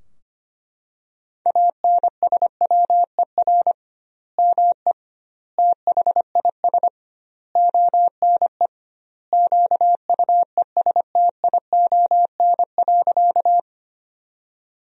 Morse code, heard as a keyed tone, about 25 wpm, 700 Hz. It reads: ANSWER ME THIS ONE QUESTION.
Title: 2x Word-Spacing - Sentences from Top 400 Words - 25wpm